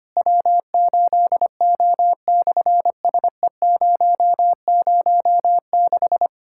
Morse code, heard as a keyed tone, about 25 wpm, 700 Hz. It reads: W8O/SE006
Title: SOTA QSO Practice 25wpm